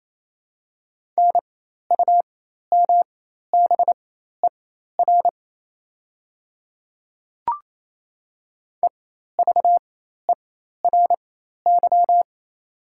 Morse code, heard as a keyed tone, about 27 wpm, spaced as for 13 wpm, 700 Hz. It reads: NUMBER EVERY
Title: 4x Character Spacing - Top 300 Words - 28wpm